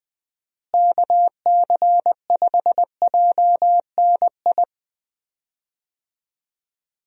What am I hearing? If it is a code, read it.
Morse code, heard as a keyed tone, about 20 wpm, 700 Hz. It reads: KC5JNI
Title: USA Call Signs - 6 Characters - Repeated 3x - 20wpm